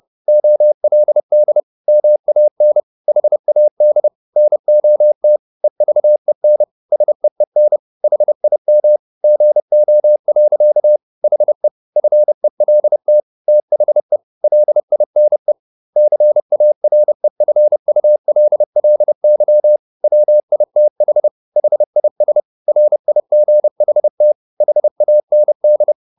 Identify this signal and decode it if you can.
Morse code, heard as a keyed tone, about 30 wpm, 600 Hz. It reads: OLD MAN HAD NOT EVEN SEEN HIM GO. HE FELT THE LINE CAREFULLY WITH HIS RIGHT HAND
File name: the old man and the sea / the old man and the sea 1 30wpm